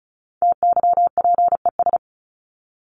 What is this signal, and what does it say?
Morse code, heard as a keyed tone, about 34 wpm, 700 Hz. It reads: TYPES
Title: ES:S Suffix - Top 100 Words with Base Words 35wpm